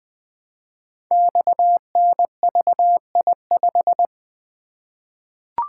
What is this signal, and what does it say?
Morse code, heard as a keyed tone, about 20 wpm, 700 Hz. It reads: XNVI5
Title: Code Groups - 5 Letter:Numbers - 20wpm